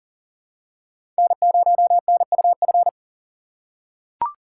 Morse code, heard as a keyed tone, about 40 wpm, 700 Hz. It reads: N0NUF
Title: USA Call Signs - 4x Character Spacing - 40wpm